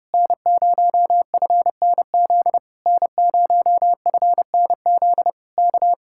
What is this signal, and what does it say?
Morse code, heard as a keyed tone, about 29 wpm, 700 Hz. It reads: N0FNZ N0FNZ K